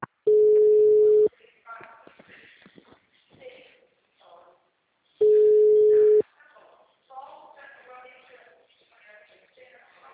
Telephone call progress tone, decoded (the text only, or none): ringback tone